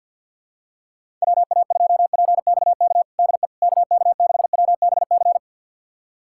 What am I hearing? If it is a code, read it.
Morse code, heard as a keyed tone, about 49 wpm, 700 Hz. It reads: WA2PXK DE KK6WBC
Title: USA Callsign DE Callsign - 6 Characters - 50wpm